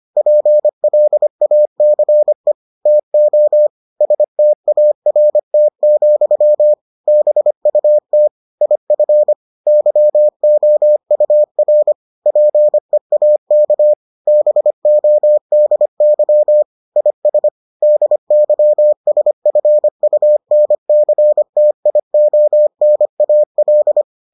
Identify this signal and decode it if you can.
Morse code, heard as a keyed tone, about 25 wpm, 600 Hz. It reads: PLACE TO START, BUT IF YOUR PEAK BODY IS DYSFUNCTIONAL